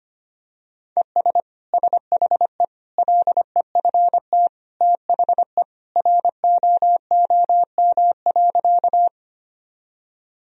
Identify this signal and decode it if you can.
Morse code, heard as a keyed tone, about 25 wpm, 700 Hz. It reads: ES SHE LEFT THE ROOM.
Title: Sentences from Top 300 Words and Abbreviations 25wpm